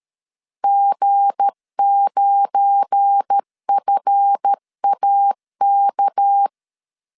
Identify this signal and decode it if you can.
Morse code, heard as a keyed tone, about 13 wpm, 800 Hz. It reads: G9FAK
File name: LEZIONI / 8 YJ45 / 8 YJ4 12wpm 2151